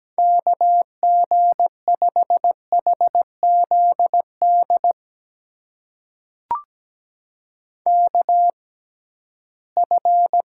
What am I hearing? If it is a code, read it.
Morse code, heard as a keyed tone, about 17 wpm, spaced as for 13 wpm, 700 Hz. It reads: KG5HZD KF
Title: USA Call Signs - 6 Characters - 6x Character Spacing - 17wpm